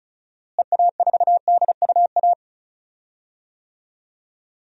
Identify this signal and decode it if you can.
Morse code, heard as a keyed tone, about 34 wpm, 700 Hz. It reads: EA4DUA
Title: DXCC-100 Call Signs - 6 Characters - Repeated 3x - 35wpm